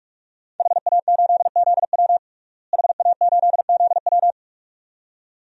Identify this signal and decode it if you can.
Morse code, heard as a keyed tone, about 46 wpm, 700 Hz. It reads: HA8ZW HA8ZW
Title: DXCC-100 Call Signs - 5 Characters - Repeated 2x - 45wpm